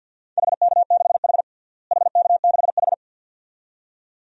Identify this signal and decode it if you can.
Morse code, heard as a keyed tone, about 49 wpm, 700 Hz. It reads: HK6H HK6H
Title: DXCC-100 Call Signs - 4 Characters - Repeated 2x - 50wpm